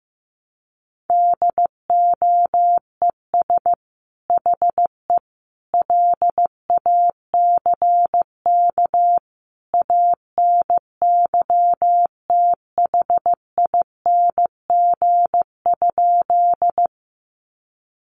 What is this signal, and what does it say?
Morse code, heard as a keyed tone, about 15 wpm, 700 Hz. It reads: DOES HE LACK ANYTHING?
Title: Sentences from Top 1100 Words 15wpm